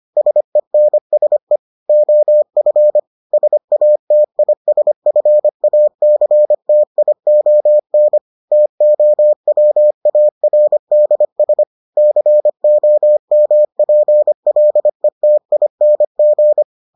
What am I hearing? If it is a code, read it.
Morse code, heard as a keyed tone, about 25 wpm, 600 Hz. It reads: SENSE OF SATISFACTION TOWARDS COMPLETING